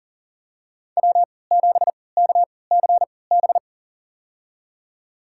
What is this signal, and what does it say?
Morse code, heard as a keyed tone, about 40 wpm, spaced as for 23 wpm, 700 Hz. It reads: W7KCB
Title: USA Call Signs - 5 Characters - 3x Character Spacing - 40wpm